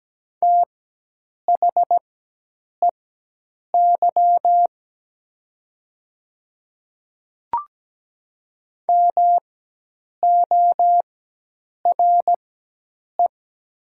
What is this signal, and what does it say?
Morse code, heard as a keyed tone, about 17 wpm, spaced as for 8 wpm, 700 Hz. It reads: THEY MORE
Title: 4x Character Spacing - Top 100 Words - 17wpm